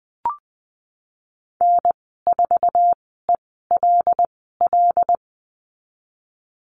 Morse code, heard as a keyed tone, about 20 wpm, spaced as for 14 wpm, 700 Hz. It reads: N4ELL
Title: 128 Lesson 21g - Number 6 - Call Signs 20wpm